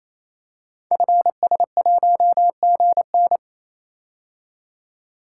Morse code, heard as a keyed tone, about 27 wpm, 700 Hz. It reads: FS1GN
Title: Code Groups - 5 Letter:Numbers - 28wpm